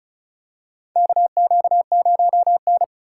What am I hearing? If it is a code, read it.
Morse code, heard as a keyed tone, about 34 wpm, 700 Hz. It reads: KQ0N
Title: USA Call Signs - 4 Characters - 3x Character Spacing - 35wpm